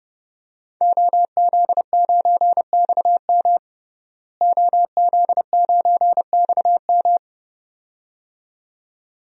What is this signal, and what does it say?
Morse code, heard as a keyed tone, about 30 wpm, 700 Hz. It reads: OZ9XM OZ9XM
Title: DXCC-100 Call Signs - 5 Characters - Repeated 2x - 30wpm